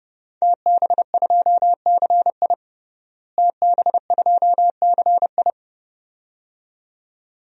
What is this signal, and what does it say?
Morse code, heard as a keyed tone, about 30 wpm, 700 Hz. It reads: TB2CI TB2CI
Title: DXCC-100 Call Signs - 5 Characters - Repeated 2x - 30wpm